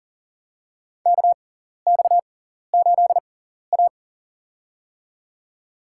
Morse code, heard as a keyed tone, about 37 wpm, spaced as for 14 wpm, 700 Hz. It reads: KX8A
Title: USA Call Signs - 4 Characters - 6x Character Spacing - 40wpm